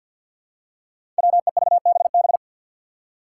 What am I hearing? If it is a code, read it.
Morse code, heard as a keyed tone, about 45 wpm, 700 Hz. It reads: WEVBB
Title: Five Challenging Letter-Number Combinations 50wpm